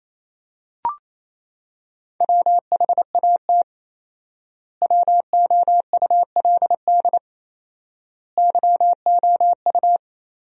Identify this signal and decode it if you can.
Morse code, heard as a keyed tone, about 28 wpm, 700 Hz. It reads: WHAT WOULD YOU
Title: 4x Word-Spacing - Sentences from Top 200 Words - 28wpm